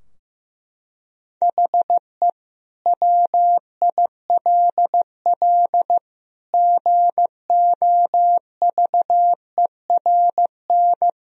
Morse code, heard as a keyed tone, about 15 wpm, 700 Hz. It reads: HE WILL GOVERN
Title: Sentences from Top 1000 Words - Review Part 2 - 15wpm